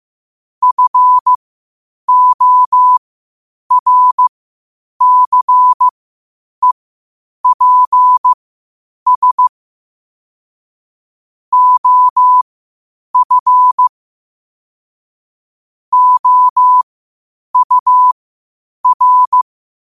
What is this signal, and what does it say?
Morse code, heard as a keyed tone, about 15 wpm, spaced as for 8 wpm, 1000 Hz. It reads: FORCEPS OF OUR
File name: qotdfiles / 8wpm